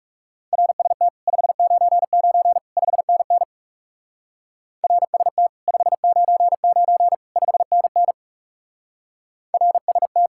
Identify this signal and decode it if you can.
Morse code, heard as a keyed tone, about 36 wpm, 700 Hz. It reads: RST 599 5NN RST 599 5NN RST
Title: Speed-Racing Common QSO Phrases - 30wpm